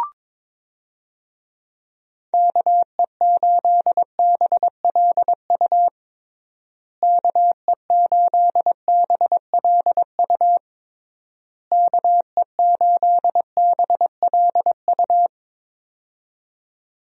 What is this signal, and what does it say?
Morse code, heard as a keyed tone, about 22 wpm, 700 Hz. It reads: KE8BLU KE8BLU KE8BLU
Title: USA Call Signs - 6 Characters - Repeated 3x - 22wpm